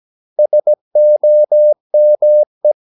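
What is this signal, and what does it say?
Morse code, heard as a keyed tone, about 17 wpm, 600 Hz. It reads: SOME
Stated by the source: 17 wpm Morse practice sending the English text of a novel.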